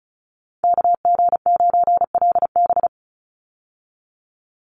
Morse code, heard as a keyed tone, about 34 wpm, 700 Hz. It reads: KG9LB
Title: Call Signs Repeated 3x - 35wpm